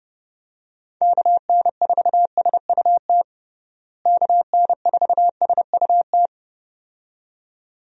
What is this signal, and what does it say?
Morse code, heard as a keyed tone, about 30 wpm, 700 Hz. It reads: KN4SUT KN4SUT
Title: USA Call Signs - 6 Characters - Repeated 2x - 30wpm